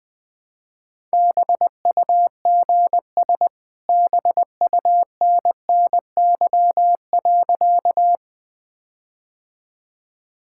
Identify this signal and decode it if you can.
Morse code, heard as a keyed tone, about 20 wpm, 700 Hz. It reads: BUGS BUNNY.
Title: All-Age Appropriate Jokes - 20wpm